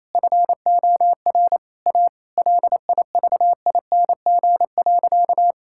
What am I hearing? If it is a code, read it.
Morse code, heard as a keyed tone, about 28 wpm, 700 Hz. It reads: FOR A LIVING.
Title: Sentences from Top 1000 Words - Review Part 6 - 28wpm